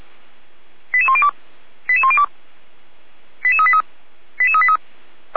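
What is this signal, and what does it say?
Five-tone selective call (ZVEI), heard as quick runs of five tones, tones of about 70 ms. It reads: 88102 88102 88383 88383
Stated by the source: ZVEI five-tone selective call received via SDR.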